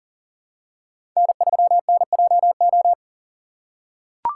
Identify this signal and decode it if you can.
Morse code, heard as a keyed tone, about 38 wpm, 700 Hz. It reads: N3NJO
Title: USA Call Signs - 5 Characters - 3x Character Spacing - 40wpm